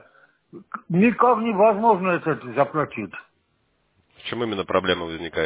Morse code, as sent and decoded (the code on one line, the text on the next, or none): none